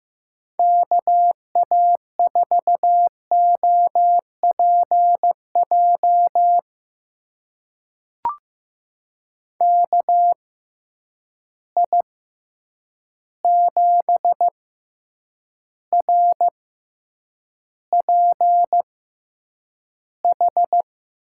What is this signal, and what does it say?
Morse code, heard as a keyed tone, about 15 wpm, spaced as for 8 wpm, 700 Hz. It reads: KA4OPJ KI7RPH
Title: USA Call Signs - 6 Characters - 6x Character Spacing - 15wpm